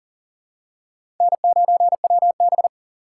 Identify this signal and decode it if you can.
Morse code, heard as a keyed tone, about 40 wpm, 700 Hz. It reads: N9WB
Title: USA Call Signs - 4 Characters - 3x Character Spacing - 40wpm